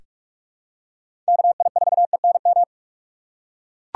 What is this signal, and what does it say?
Morse code, heard as a keyed tone, about 44 wpm, 700 Hz. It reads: KI4ENM
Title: USA Call Signs - 6x Character Spacing - 45wpm